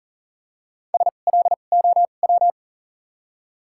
Morse code, heard as a keyed tone, about 40 wpm, spaced as for 29 wpm, 700 Hz. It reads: SLOW
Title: 2x Character Spacing - Top 500 Words - 40wpm